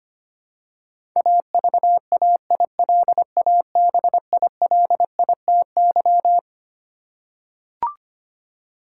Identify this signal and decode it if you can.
Morse code, heard as a keyed tone, about 25 wpm, 700 Hz. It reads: AVAILABILITY
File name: Twelve Letter Words - 25wpm